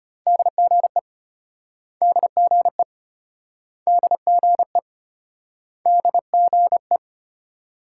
Morse code, heard as a keyed tone, about 33 wpm, 700 Hz. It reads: DGE DGE DGE DGE